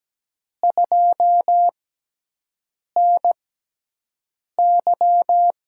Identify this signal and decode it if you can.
Morse code, heard as a keyed tone, about 17 wpm, spaced as for 6 wpm, 700 Hz. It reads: 2NY
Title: USA Call Signs - 6x Character Spacing - 17wpm